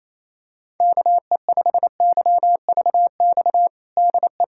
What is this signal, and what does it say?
Morse code, heard as a keyed tone, about 28 wpm, 700 Hz. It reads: KE5YVX DE
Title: USA Callsign DE Callsign - 6 Characters - 28wpm